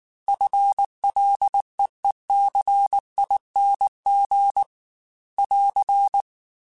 Morse code, heard as a keyed tone, about 19 wpm, 800 Hz. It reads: FLEECING <AR>
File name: FL / FL mp3s / FL Prefixes 1x1 @18